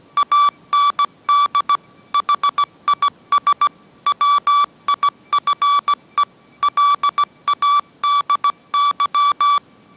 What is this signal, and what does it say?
Morse code, heard as a keyed tone, about 17 wpm, 1150 Hz. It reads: AND HIS WIFE LADY